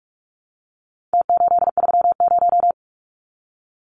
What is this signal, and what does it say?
Morse code, heard as a keyed tone, about 44 wpm, 700 Hz. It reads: T830
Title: Numbers - Contest Style - 4-Digit 45wpm